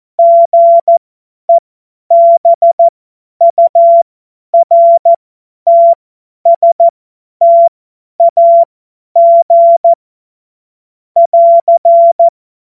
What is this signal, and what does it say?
Morse code, heard as a keyed tone, about 14 wpm, spaced as for 10 wpm, 680 Hz. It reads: GEBURTSTAG <AR>
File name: Morsekurs / Witze / Witz 09. Februar 2021, 50 F70